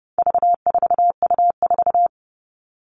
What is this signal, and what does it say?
Morse code, heard as a keyed tone, about 30 wpm, 700 Hz. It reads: V4U4
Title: All Cut Numbers - 4-Digit 30wpm